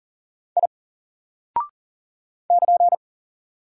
Morse code, heard as a keyed tone, about 39 wpm, 700 Hz. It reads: I <KN>
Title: Single Character (Letter:Number:Punctuation:Prosign) 40wpm